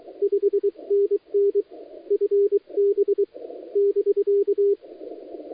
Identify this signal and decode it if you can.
Morse code, heard as a keyed tone, about 23 wpm, 400 Hz. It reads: 5NN FB <BK>
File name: dlazaawansowanych / nagraniazpasma / titawaNETcontrol